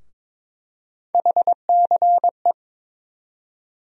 Morse code, heard as a keyed tone, about 22 wpm, 700 Hz. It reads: HCE
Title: USA Call Sign Postfixes - 3 Letters 22wpm